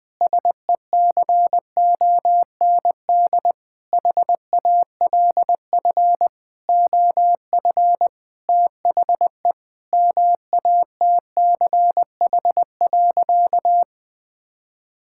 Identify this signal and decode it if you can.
Morse code, heard as a keyed tone, about 20 wpm, 700 Hz. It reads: SECOND HALF OF THE MATCH.
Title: Sentences with 5-Word Phrases - 20wpm